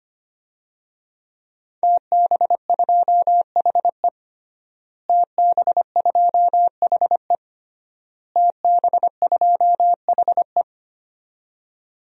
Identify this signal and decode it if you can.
Morse code, heard as a keyed tone, about 25 wpm, 700 Hz. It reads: TB2HE TB2HE TB2HE
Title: DXCC-100 Call Signs - 5 Characters - Repeated 3x - 25wpm